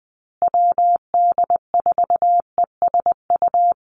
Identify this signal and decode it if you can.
Morse code, heard as a keyed tone, about 20 wpm, 700 Hz. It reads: WD4ESU
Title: USA Call Signs Repeated 3x 20wpm